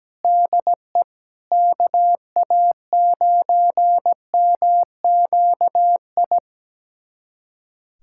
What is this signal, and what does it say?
Morse code, heard as a keyed tone, about 17 wpm, 700 Hz. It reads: DE KA9MQI
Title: USA Callsign DE Callsign - 6 Characters - 17wpm